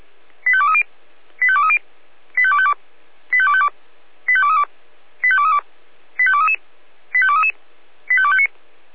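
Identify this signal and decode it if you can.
Five-tone selective call (ZVEI), heard as quick runs of five tones, tones of about 70 ms. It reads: 85230 85230 85252 85252 85232 85232 85233 85233 85259
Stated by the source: ZVEI five-tone selective call 85885 received on SDR.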